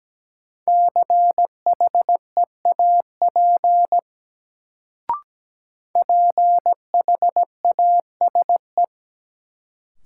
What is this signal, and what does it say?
Morse code, heard as a keyed tone, about 17 wpm, 700 Hz. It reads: CHEAP PHASE
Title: Sentences from Top 1200 Words - New Word Drill - 17wpm